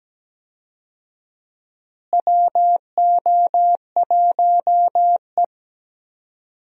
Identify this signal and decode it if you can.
Morse code, heard as a keyed tone, about 17 wpm, 700 Hz. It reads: WO1E
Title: USA Call Signs - 4 Characters - Repeated 2x - 17wpm